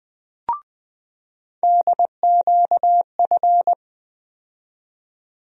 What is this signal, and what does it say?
Morse code, heard as a keyed tone, about 21 wpm, 700 Hz. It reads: DQF
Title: USA Call Sign Postfixes - 3 Letters 20wpm